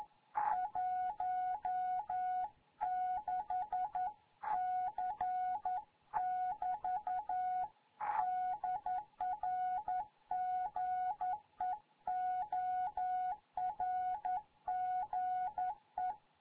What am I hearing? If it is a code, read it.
Morse code, heard as a keyed tone, about 11 wpm, 740 Hz. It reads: 16 C <BT> DR GEORGE
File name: Morze-HA2MN